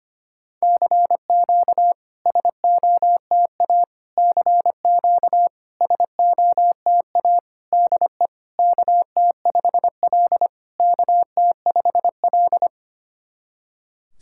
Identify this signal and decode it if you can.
Morse code, heard as a keyed tone, about 25 wpm, 700 Hz. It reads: CQ SOTA CQ SOTA DE KT5L KT5L
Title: SOTA QSO Practice 25wpm